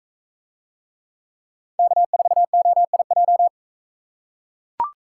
Morse code, heard as a keyed tone, about 42 wpm, 700 Hz. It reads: K4OIJ